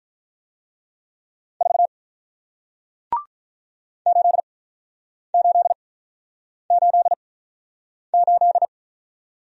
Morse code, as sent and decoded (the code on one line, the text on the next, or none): ....- / ---.. ---.. ---.. ---..
4 8888